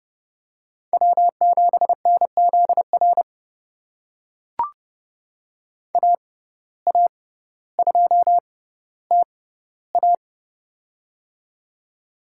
Morse code, heard as a keyed tone, about 30 wpm, spaced as for 15 wpm, 700 Hz. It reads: W7NZR AA2TA
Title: USA Call Signs - 5 Characters - 6x Character Spacing - 30wpm